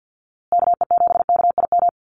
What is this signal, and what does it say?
Morse code, heard as a keyed tone, about 50 wpm, 700 Hz. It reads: KE7KIM
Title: USA Call Signs Repeated 3x 50wpm